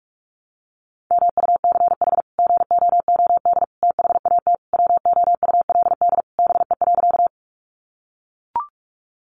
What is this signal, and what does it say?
Morse code, heard as a keyed tone, about 44 wpm, 700 Hz. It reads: MUCH GOOD THAT WOULD BE.
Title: Sentences from Top 200 Words 45wpm